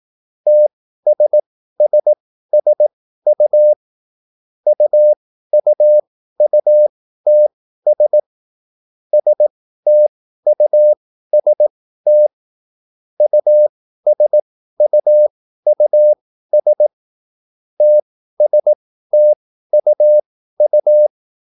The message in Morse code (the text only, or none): TSSSU UUUTS STUST USUUS TSTUU